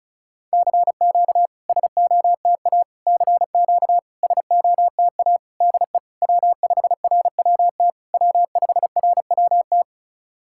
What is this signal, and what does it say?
Morse code, heard as a keyed tone, about 36 wpm, 700 Hz. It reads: CQ SOTA CQ SOTA DE W5RWT W5RWT
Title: SOTA QSO Practice 35wpm